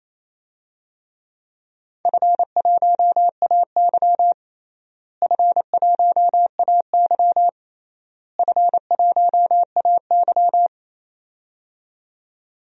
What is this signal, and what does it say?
Morse code, heard as a keyed tone, about 28 wpm, 700 Hz. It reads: F1AY F1AY F1AY
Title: DXCC-100 Call Signs - 4 Characters - Repeated 3x - 28wpm